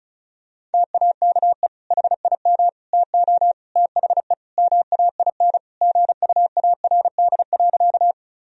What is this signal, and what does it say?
Morse code, heard as a keyed tone, about 36 wpm, 700 Hz. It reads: TAKE HIM TO THE MAIN GUARD.
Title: Sentences from Top 1000 Words - Review Part 4 - 35wpm